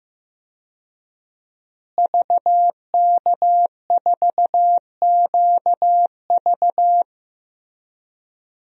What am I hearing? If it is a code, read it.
Morse code, heard as a keyed tone, about 15 wpm, 700 Hz. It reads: VK4QV
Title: DXCC-100 Call Signs - 5 Characters - Repeated 3x - 15wpm